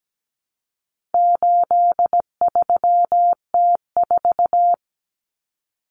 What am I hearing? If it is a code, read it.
Morse code, heard as a keyed tone, about 17 wpm, 700 Hz. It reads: 83T4
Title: Common Cut Numbers - 4-Digit 17wpm